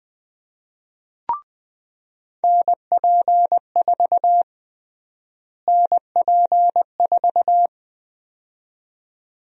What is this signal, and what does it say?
Morse code, heard as a keyed tone, about 21 wpm, 700 Hz. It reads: NP4 NP4
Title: USA Call Sign Prefixes - Repeated 2x - 20wpm